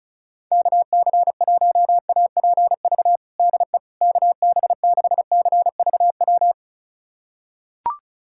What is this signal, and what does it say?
Morse code, heard as a keyed tone, about 36 wpm, 700 Hz. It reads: KC1APV DE KB6CVW
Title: USA Callsign DE Callsign 35wpm